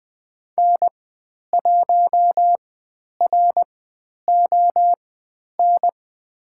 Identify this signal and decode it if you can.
Morse code, heard as a keyed tone, about 20 wpm, spaced as for 10 wpm, 700 Hz. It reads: N1RON
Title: 109 Lesson 18g - Letter m - Call Signs 20wpm